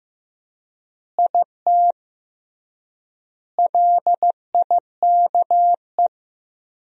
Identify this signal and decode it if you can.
Morse code, heard as a keyed tone, about 15 wpm, 700 Hz. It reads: IT LIKE
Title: 3x Word-Spacing - Sentences from Top 300 Words - 15wpm